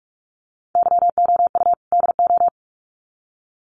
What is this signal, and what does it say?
Morse code, heard as a keyed tone, about 44 wpm, 700 Hz. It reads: YOU DO
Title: Sets of 2 Words 45wpm